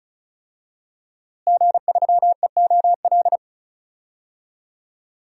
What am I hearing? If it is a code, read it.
Morse code, heard as a keyed tone, about 36 wpm, 700 Hz. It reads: G3EOL
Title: DXCC-100 Call Signs - 5 Characters - Repeated 3x - 35wpm